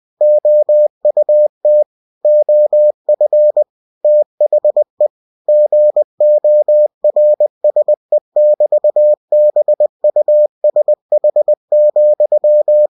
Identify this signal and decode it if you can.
Morse code, heard as a keyed tone, about 20 wpm, 600 Hz. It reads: OUT OF THE GORSE<BT>BUSH,
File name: cw-books / winnie-the-pooh / winnie-20wpm-page-0004